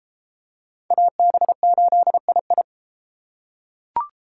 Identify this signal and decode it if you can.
Morse code, heard as a keyed tone, about 30 wpm, 700 Hz. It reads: AB8II